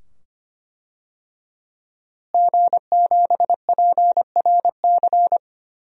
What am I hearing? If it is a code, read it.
Morse code, heard as a keyed tone, about 25 wpm, 700 Hz. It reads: G7PRC
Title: DXCC-100 Call Signs - 5 Characters - Repeated 2x - 25wpm